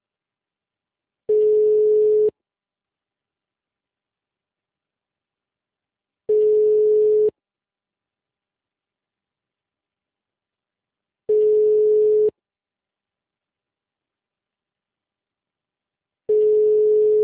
Ringback tone (Europe, 425 Hz).